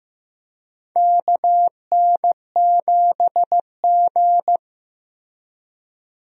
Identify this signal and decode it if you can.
Morse code, heard as a keyed tone, about 15 wpm, 700 Hz. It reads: KN7G
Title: USA Call Signs - 4 Characters - Repeated 2x - 15wpm